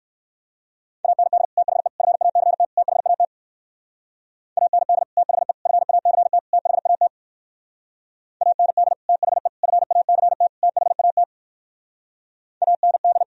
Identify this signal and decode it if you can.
Morse code, heard as a keyed tone, about 59 wpm, 700 Hz. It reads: AND THE FACT THAT AND THE FACT THAT AND THE FACT THAT AND